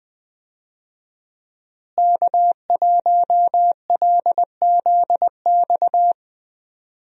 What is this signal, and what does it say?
Morse code, heard as a keyed tone, about 20 wpm, 700 Hz. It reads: K1LZX
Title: USA Call Signs - 5 Characters - Repeated 2x - 20wpm